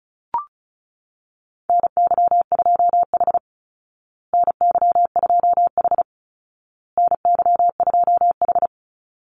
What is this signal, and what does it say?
Morse code, heard as a keyed tone, about 34 wpm, 700 Hz. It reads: NY2H NY2H NY2H
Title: Call Signs Repeated 2x - 35wpm